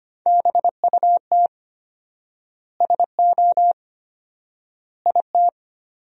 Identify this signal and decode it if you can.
Morse code, heard as a keyed tone, about 25 wpm, 700 Hz. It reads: BUT SO IT
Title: 4x Word-Spacing - Sentences from Top 100 Words - 25wpm